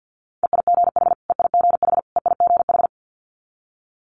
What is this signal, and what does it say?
Morse code, heard as a keyed tone, about 50 wpm, 690 Hz. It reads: EIGH EIGH EIGH
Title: Multi-letter Phonemes Repeated 3x 50wpm